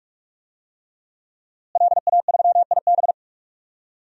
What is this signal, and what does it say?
Morse code, heard as a keyed tone, about 46 wpm, 700 Hz. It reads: LA3IB
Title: DXCC-100 Call Signs - 5 Characters - Repeated 3x - 45wpm